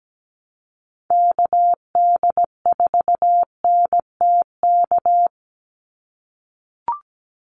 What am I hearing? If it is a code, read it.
Morse code, heard as a keyed tone, about 17 wpm, 700 Hz. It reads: KD4NTK
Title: USA Call Signs 17wpm